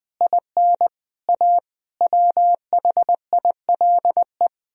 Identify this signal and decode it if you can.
Morse code, heard as a keyed tone, about 21 wpm, 700 Hz. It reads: IN A WHILE
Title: Speed-Racing - Sets of 5 Words 20wpm